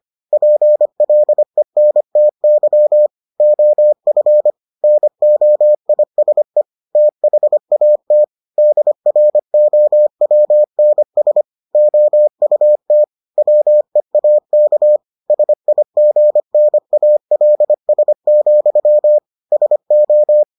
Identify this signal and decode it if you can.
Morse code, heard as a keyed tone, about 25 wpm, 600 Hz. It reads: PLENTY OF NOISE THAT DROWNS OUT WEAK SIGNALS, SO